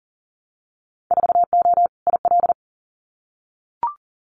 Morse code, heard as a keyed tone, about 40 wpm, 700 Hz. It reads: 4O IL
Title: Winter Field Day Exchanges Encore 40wpm